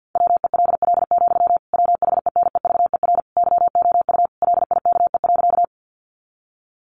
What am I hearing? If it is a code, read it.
Morse code, heard as a keyed tone, about 50 wpm, 700 Hz. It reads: WELL, WHENEVER YOU LIKE.
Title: Sentences from Top 1200 Words 50wpm